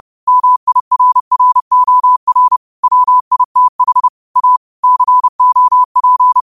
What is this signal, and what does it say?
Morse code, heard as a keyed tone, about 30 wpm, 1000 Hz. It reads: MIRROR WITH A COP
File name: qotdfiles / 30wpm